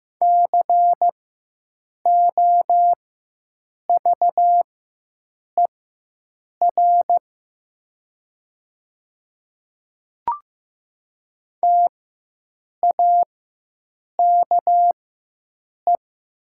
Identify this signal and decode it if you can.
Morse code, heard as a keyed tone, about 15 wpm, spaced as for 7 wpm, 700 Hz. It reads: COVER TAKE
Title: 4x Character Spacing - Top 500 Words - 15wpm